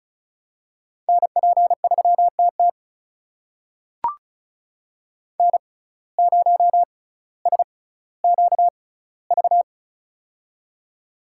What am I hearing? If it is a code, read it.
Morse code, heard as a keyed tone, about 36 wpm, spaced as for 18 wpm, 700 Hz. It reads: NP3TT N0SQV
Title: USA Call Signs - 5 Characters - 6x Character Spacing - 35wpm